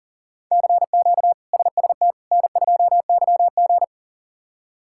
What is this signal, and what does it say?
Morse code, heard as a keyed tone, about 40 wpm, 700 Hz. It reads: CQ SST N2YG
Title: K1USN SST QSO Practice - 40wpm